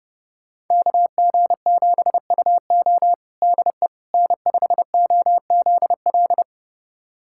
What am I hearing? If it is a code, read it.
Morse code, heard as a keyed tone, about 32 wpm, 700 Hz. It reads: KG7UO DE N5OZL
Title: USA Callsign DE Callsign - 5 Characters - 30wpm